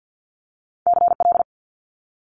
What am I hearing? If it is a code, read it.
Morse code, heard as a keyed tone, about 50 wpm, 700 Hz. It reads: CL
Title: Multi-letter Phonemes Quick Fire 50wpm